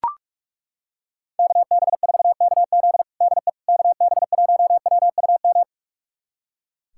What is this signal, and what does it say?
Morse code, heard as a keyed tone, about 41 wpm, 700 Hz. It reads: KB4KZ DE KB1WUM
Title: USA Callsign DE Callsign 45wpm